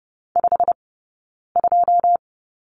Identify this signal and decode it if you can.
Morse code, heard as a keyed tone, about 30 wpm, spaced as for 9 wpm, 690 Hz. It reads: 52
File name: Lesson 19d - Letter w - Two Characters 30wpm